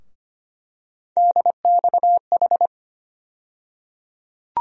Morse code, heard as a keyed tone, about 25 wpm, 700 Hz. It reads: DXH
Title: USA Call Sign Postfixes - 3 Letters 25wpm